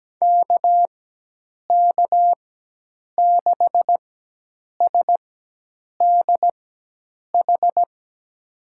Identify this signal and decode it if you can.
Morse code, heard as a keyed tone, about 17 wpm, spaced as for 8 wpm, 700 Hz. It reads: KK6SDH
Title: USA Call Signs - 6 Characters - 4x Character Spacing - 17wpm